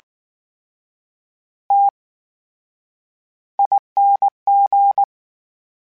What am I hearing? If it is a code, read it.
Morse code, heard as a keyed tone, about 19 wpm, spaced as for 9 wpm, 800 Hz. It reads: TING
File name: Special INGs / Special INGs-mp3s / VerySlowToFaster Ing @18